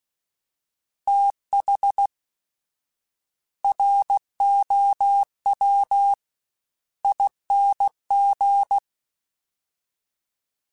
Morse code, heard as a keyed tone, about 16 wpm, spaced as for 11 wpm, 800 Hz. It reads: THROWING
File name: TH / TH mp3s / Fast TH @15